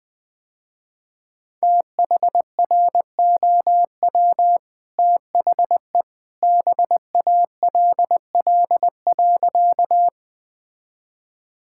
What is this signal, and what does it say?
Morse code, heard as a keyed tone, about 21 wpm, 700 Hz. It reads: THROW THE BALL.
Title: Easy Sentences with Hints - 20wpm